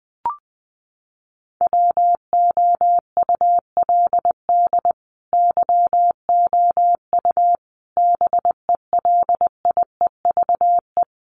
Morse code, heard as a keyed tone, about 20 wpm, 700 Hz. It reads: WOULD YOU BELIEVE
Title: Sentences from Top 1300 Words 20wpm